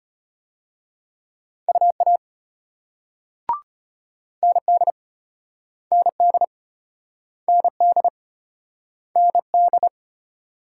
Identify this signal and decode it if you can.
Morse code, heard as a keyed tone, about 34 wpm, 700 Hz. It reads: UA ND ND ND ND